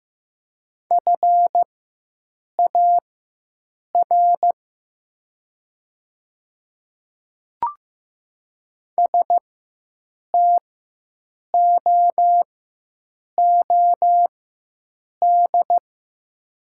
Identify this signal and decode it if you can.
Morse code, heard as a keyed tone, about 15 wpm, spaced as for 7 wpm, 700 Hz. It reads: FAR STOOD
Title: 4x Character Spacing - Top 500 Words - 15wpm